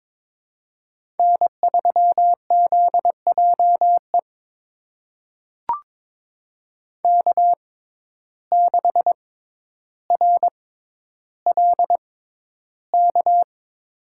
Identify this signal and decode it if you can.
Morse code, heard as a keyed tone, about 22 wpm, spaced as for 11 wpm, 700 Hz. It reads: N3ZJE K6RLK
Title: USA Call Signs - 5 Characters - 6x Character Spacing - 22wpm